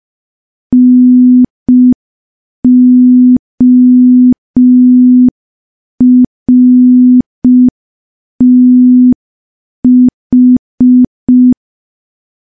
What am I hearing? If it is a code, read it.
Morse code, heard as a keyed tone, about 5 wpm, 250 Hz. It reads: NORTH
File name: morse / morse-training-05WPM